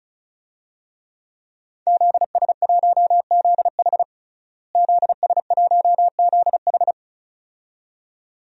Morse code, heard as a keyed tone, about 34 wpm, 700 Hz. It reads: ZS1ZH ZS1ZH
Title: DXCC-100 Call Signs - 5 Characters - Repeated 2x - 35wpm